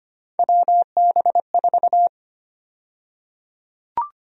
Morse code, heard as a keyed tone, about 25 wpm, 700 Hz. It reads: WB4
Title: USA Call Sign Prefixes - Repeated 2x - 25wpm